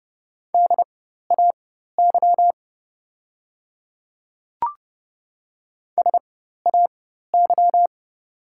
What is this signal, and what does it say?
Morse code, heard as a keyed tone, about 30 wpm, spaced as for 14 wpm, 700 Hz. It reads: DAY SAY
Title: 4x Character Spacing - Top 200 Words - 30wpm